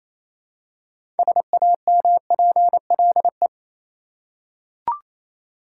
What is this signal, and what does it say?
Morse code, heard as a keyed tone, about 28 wpm, 700 Hz. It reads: SAMPLE
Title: Six Letter Words - 28wpm